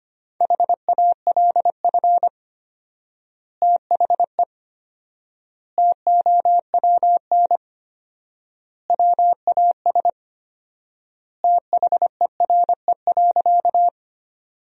Morse code, heard as a keyed tone, about 25 wpm, 700 Hz. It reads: HALF THE TOWN WAS THERE.
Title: 4x Word-Spacing - Sentences from Top 500 Words - 25wpm